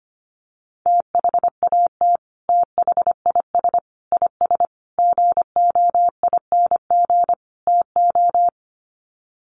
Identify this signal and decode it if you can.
Morse code, heard as a keyed tone, about 25 wpm, 700 Hz. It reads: THAT THIS IS GOING TO
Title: Sets of 5 Words Encore2 25wpm